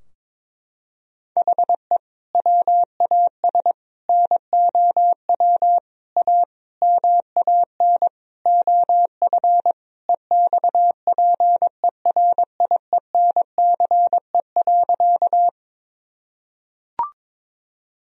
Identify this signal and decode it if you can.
Morse code, heard as a keyed tone, about 22 wpm, 700 Hz. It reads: HE WAS NOW A MAN OF EXPERIENCE.
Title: Sentences from Top 1000 Words - Review Part 3 - 22wpm